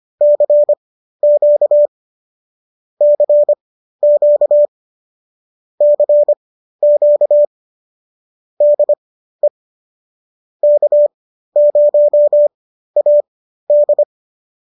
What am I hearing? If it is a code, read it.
Morse code, heard as a keyed tone, about 25 wpm, spaced as for 13 wpm, 590 Hz. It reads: CQ CQ CQ DE K0AD
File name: qso213 13